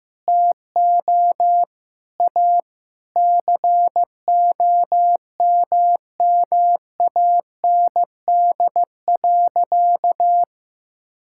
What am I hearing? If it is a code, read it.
Morse code, heard as a keyed tone, about 15 wpm, 700 Hz. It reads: TO A COMMAND.
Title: Sentences from Top 1600 Words 15wpm